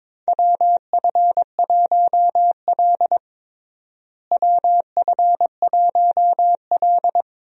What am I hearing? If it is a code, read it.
Morse code, heard as a keyed tone, about 22 wpm, 700 Hz. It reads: WF1L WF1L
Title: USA Call Signs - 4 Characters - Repeated 3x - 22wpm